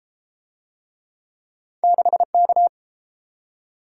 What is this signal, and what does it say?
Morse code, heard as a keyed tone, about 32 wpm, 700 Hz. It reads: 6K